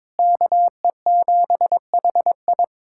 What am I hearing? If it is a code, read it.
Morse code, heard as a keyed tone, about 22 wpm, 700 Hz. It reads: KE7HI
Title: USA Callsign DE Callsign - 5 Characters - 22wpm